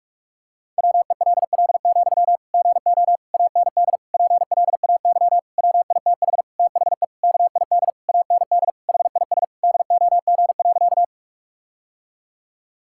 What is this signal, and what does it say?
Morse code, heard as a keyed tone, about 46 wpm, 700 Hz. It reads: WELL, GO AND PLAY WITH THE KID AND HIS DOG.
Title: Sentences from Top 1600 Words 45wpm